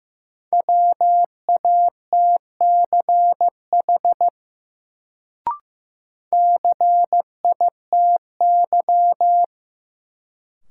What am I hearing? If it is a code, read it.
Morse code, heard as a keyed tone, about 15 wpm, 700 Hz. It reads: WATCH CITY
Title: Sentences from Top 300 Words - New Word Drill - 15wpm